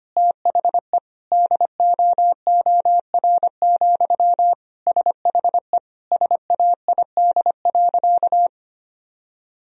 Morse code, heard as a keyed tone, about 25 wpm, 700 Hz. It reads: THE DOOR, SHE SAID.
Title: Sentences from Top 1000 Words - Review Part 1 - 25wpm